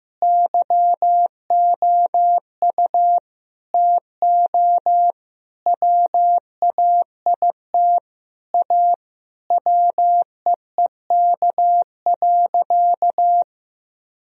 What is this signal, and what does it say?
Morse code, heard as a keyed tone, about 15 wpm, 700 Hz. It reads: YOU TO WAIT A WEEK.
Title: Sentences from Top 1000 Words - Review Part 2 - 15wpm